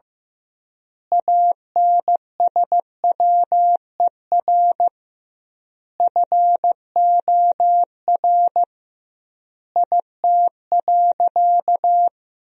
Morse code, heard as a keyed tone, about 15 wpm, 700 Hz. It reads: ANSWER FOR IT.
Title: 2x Word-Spacing - Sentences from Top 300 Words - 15wpm